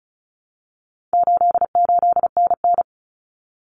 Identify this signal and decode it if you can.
Morse code, heard as a keyed tone, about 34 wpm, 700 Hz. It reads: 88NN
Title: Common Cut Numbers - 4-Digit 35wpm